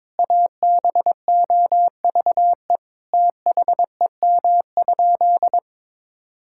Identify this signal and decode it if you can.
Morse code, heard as a keyed tone, about 21 wpm, 700 Hz. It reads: ABOVE THEM?
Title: Sentences from Top 1000 Words - Review Part 5 - 22wpm